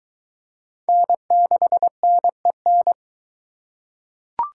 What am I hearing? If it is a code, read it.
Morse code, heard as a keyed tone, about 23 wpm, 700 Hz. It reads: N6NEN